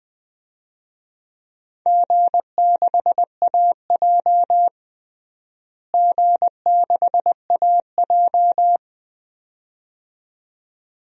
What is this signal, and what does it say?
Morse code, heard as a keyed tone, about 21 wpm, 700 Hz. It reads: G6AJ G6AJ
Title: DXCC-100 Call Signs - 4 Characters - Repeated 2x - 20wpm